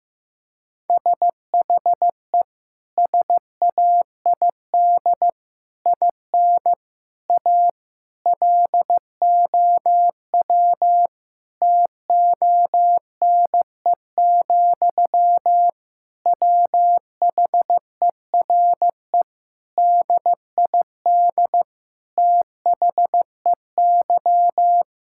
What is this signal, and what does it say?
Morse code, heard as a keyed tone, about 15 wpm, 700 Hz. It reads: SHE SAID IN A LOW TONE, WHERE DID THEY
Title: Sentences from Top 1600 Words 15wpm